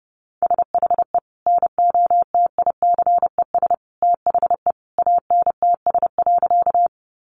Morse code, heard as a keyed tone, about 30 wpm, 700 Hz. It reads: SHE NOTICES THE ANTS.
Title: Additional Easy Phrases 30wpm